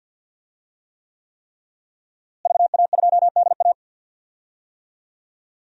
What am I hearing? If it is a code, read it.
Morse code, heard as a keyed tone, about 52 wpm, 700 Hz. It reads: VA2DA
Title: DXCC-100 Call Signs - 5 Characters - Repeated 3x - 50wpm